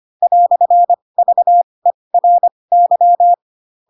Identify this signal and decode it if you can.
Morse code, heard as a keyed tone, about 25 wpm, spaced as for 20 wpm, 700 Hz. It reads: "VERY
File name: Pinocchio / Pinocchio ch7 15 25 ik1hge